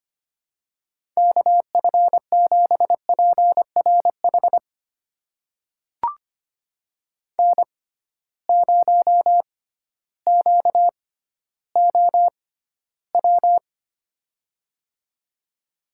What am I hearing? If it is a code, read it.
Morse code, heard as a keyed tone, about 25 wpm, spaced as for 13 wpm, 700 Hz. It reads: KF7PRH N0QOW
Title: USA Call Signs - 6x Character Spacing - 25wpm